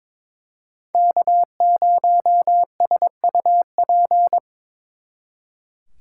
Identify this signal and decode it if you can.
Morse code, heard as a keyed tone, about 26 wpm, 700 Hz. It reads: K0SUP
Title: Speed Racing - USA 1x3 Call Signs 22wpm